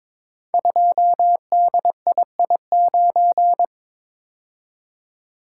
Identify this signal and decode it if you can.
Morse code, heard as a keyed tone, about 22 wpm, 700 Hz. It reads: 2DII9
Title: Code Groups - 5 Letter:Numbers - 22wpm